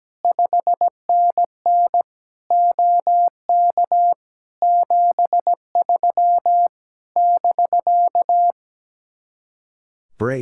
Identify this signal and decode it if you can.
Morse code, heard as a keyed tone, about 17 wpm, 700 Hz. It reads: 5NN OK 73 <BK>
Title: Speed-Racing Common QSO Phrases Encore - 17wpm